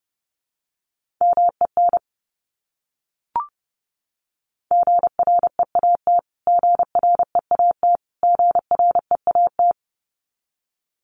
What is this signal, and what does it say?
Morse code, heard as a keyed tone, about 30 wpm, 700 Hz. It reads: MEN GREAT GREAT GREAT
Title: Top 300 Words Repeated 30wpm